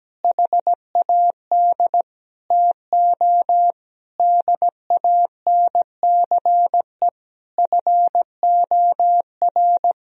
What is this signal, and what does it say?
Morse code, heard as a keyed tone, about 17 wpm, 700 Hz. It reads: HAD TO DANCE FOR
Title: Sentences from Top 1000 Words - Review Part 4 - 17wpm